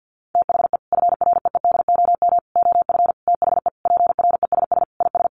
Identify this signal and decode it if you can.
Morse code, heard as a keyed tone, about 50 wpm, 700 Hz. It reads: THE FREEDOM OF THE PRESS IS
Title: Sentences from Top 800 Words 50wpm